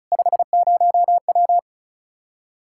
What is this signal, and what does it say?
Morse code, heard as a keyed tone, about 34 wpm, 700 Hz. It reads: 50W
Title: Practice QSO - Exchange 2 - Info - 35wpm